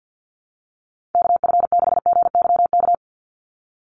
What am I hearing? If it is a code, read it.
Morse code, heard as a keyed tone, about 48 wpm, 700 Hz. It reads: KF6GYK
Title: USA Call Signs 50wpm